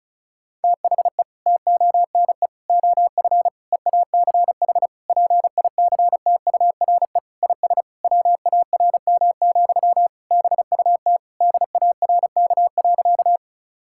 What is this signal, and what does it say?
Morse code, heard as a keyed tone, about 36 wpm, 700 Hz. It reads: THE TONE OF EACH PICTURE IS WARM, BUT DARK.
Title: Sentences from Top 1600 Words 35wpm